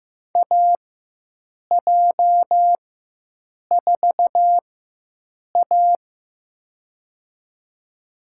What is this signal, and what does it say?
Morse code, heard as a keyed tone, about 15 wpm, spaced as for 7 wpm, 700 Hz. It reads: AJ4A
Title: USA Call Signs - 4x Character Spacing - 15wpm